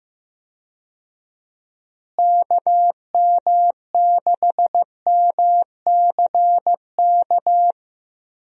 Morse code, heard as a keyed tone, about 15 wpm, 700 Hz. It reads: KM6MCK
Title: USA Call Signs - 6 Characters - Repeated 2x - 15wpm